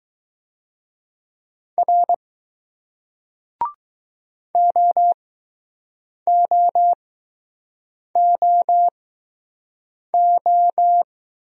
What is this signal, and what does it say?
Morse code, heard as a keyed tone, about 20 wpm, spaced as for 6 wpm, 700 Hz. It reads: R OOOO